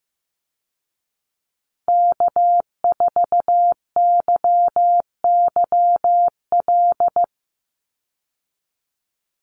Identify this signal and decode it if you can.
Morse code, heard as a keyed tone, about 15 wpm, 700 Hz. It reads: K4YYL
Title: Call Signs Repeated 2x - 15wpm